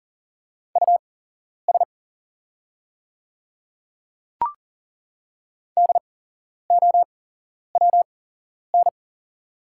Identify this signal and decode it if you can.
Morse code, heard as a keyed tone, about 40 wpm, spaced as for 11 wpm, 690 Hz. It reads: US DOWN
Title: 8x Character Spacing - Top 200 Words - 40wpm